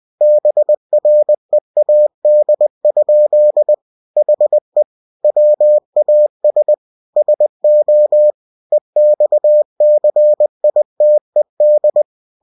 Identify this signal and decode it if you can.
Morse code, heard as a keyed tone, about 20 wpm, 600 Hz. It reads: BREAD? HE WAS SO EXCITED
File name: cw-books / winnie-the-pooh / winnie-20wpm-page-0015